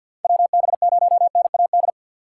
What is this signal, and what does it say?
Morse code, heard as a keyed tone, about 49 wpm, 700 Hz. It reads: WB0NAD
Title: USA Call Signs - 6 Characters - 4x Character Spacing - 50wpm